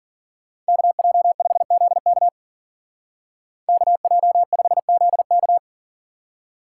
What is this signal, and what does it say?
Morse code, heard as a keyed tone, about 50 wpm, 700 Hz. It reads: KJ5ZK KJ5ZK
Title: Speed Racing - USA 2x2 Call Signs 40wpm